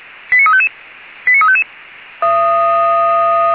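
Five-tone selective call (ZVEI), heard as quick runs of five tones, tones of about 70 ms. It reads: 89355 89355
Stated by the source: ZVEI five-tone selective call received by SDR on the Feuerwehr Orlamünde (fire brigade) audio server.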